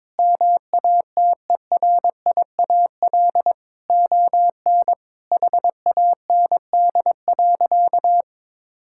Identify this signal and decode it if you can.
Morse code, heard as a keyed tone, about 22 wpm, 700 Hz. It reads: MATERIAL ON HAND.
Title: Sentences from Top 1000 Words - Review Part 5 - 22wpm